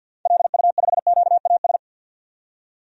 Morse code, heard as a keyed tone, about 52 wpm, 700 Hz. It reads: LU5QAS
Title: DXCC-100 Call Signs - 6 Characters - 50wpm